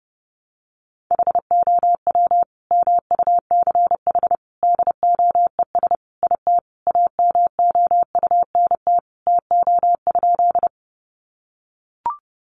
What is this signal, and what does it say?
Morse code, heard as a keyed tone, about 30 wpm, 700 Hz. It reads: HOW MUCH DOES IT AMOUNT TO?
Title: Sentences from Top 700 Words 30wpm